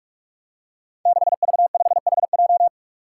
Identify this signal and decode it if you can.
Morse code, heard as a keyed tone, about 41 wpm, 700 Hz. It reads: 6V5HJ
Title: Five Challenging Letter-Number Combinations 45wpm